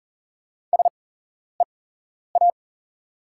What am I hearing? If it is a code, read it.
Morse code, heard as a keyed tone, about 37 wpm, spaced as for 11 wpm, 690 Hz. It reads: SEA
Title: 8x Character Spacing - Top 500 Words - 40wpm